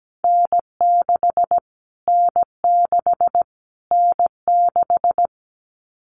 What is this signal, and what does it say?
Morse code, heard as a keyed tone, about 17 wpm, 700 Hz. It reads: N6 N6 N6
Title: Callsign Prefixes for Top 25 DXCC - Repeated 3x - 17wpm